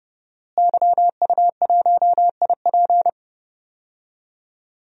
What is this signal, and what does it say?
Morse code, heard as a keyed tone, about 30 wpm, 700 Hz. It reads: YU1IP
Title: DXCC-100 Call Signs - 5 Characters - Repeated 3x - 30wpm